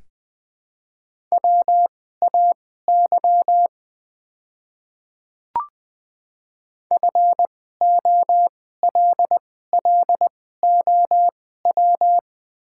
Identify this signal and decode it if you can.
Morse code, heard as a keyed tone, about 20 wpm, spaced as for 15 wpm, 700 Hz. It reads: WAY FOLLOW
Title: 2x Character Spacing - Top 200 Words - 20wpm